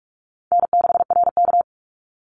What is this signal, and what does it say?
Morse code, heard as a keyed tone, about 46 wpm, 700 Hz. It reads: N6RK
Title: Call Signs - 45wpm